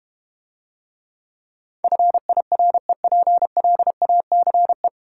Morse code, heard as a keyed tone, about 31 wpm, 700 Hz. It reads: FIREPLACE
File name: Easy Words - Speed-Racing with Hints - 25wpm